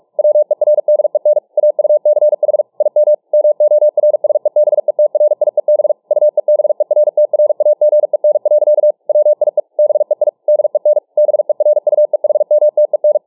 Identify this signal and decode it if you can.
Morse code, heard as a keyed tone, about 45 wpm, 600 Hz. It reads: WERDEN AUCH IM MORSEBETRIEB UEBERTRAGEN. WIE BEI DEN BERUEHMTEN